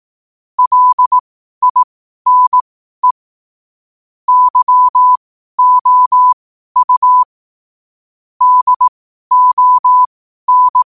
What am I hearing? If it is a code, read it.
Morse code, heard as a keyed tone, about 18 wpm, spaced as for 12 wpm, 1000 Hz. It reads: LINE YOU DON
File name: qotdfiles / 12wpm